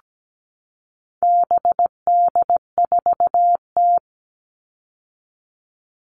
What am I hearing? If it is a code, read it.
Morse code, heard as a keyed tone, about 17 wpm, 700 Hz. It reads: BD4T
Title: All Cut Numbers - 4-Digit 17wpm